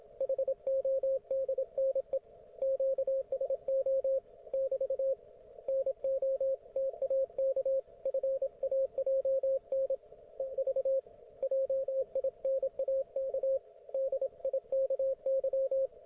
Morse code, heard as a keyed tone, about 27 wpm, 540 Hz, with radio noise. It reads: HODNE QSO <BT> NO TAK FAJN EV JINAK DIKY